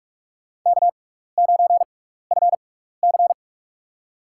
Morse code, heard as a keyed tone, about 41 wpm, spaced as for 15 wpm, 700 Hz. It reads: K9FC
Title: USA Call Signs - 4 Characters - 6x Character Spacing - 45wpm